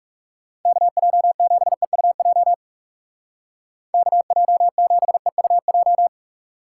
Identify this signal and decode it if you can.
Morse code, heard as a keyed tone, about 36 wpm, 700 Hz. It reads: KJ7EUJ KJ7EUJ
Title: Speed Racing - USA 2x3 Call Signs 30wpm